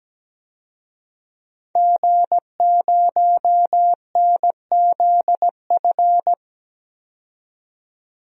Morse code, heard as a keyed tone, about 17 wpm, 700 Hz. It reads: G0NZF
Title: DXCC-100 Call Signs - 5 Characters - 17wpm